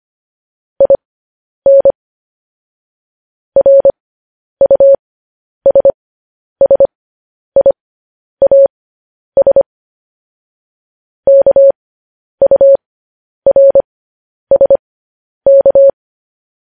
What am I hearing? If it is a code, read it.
Morse code, heard as a keyed tone, about 25 wpm, spaced as for 10 wpm, 550 Hz. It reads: IN RUSSIAS KURSK
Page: News Headlines in Morse Code at 10 WPM